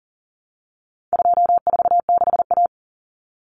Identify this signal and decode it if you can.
Morse code, heard as a keyed tone, about 40 wpm, 700 Hz. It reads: 246A
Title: Common Cut Numbers - 4-Digit 40wpm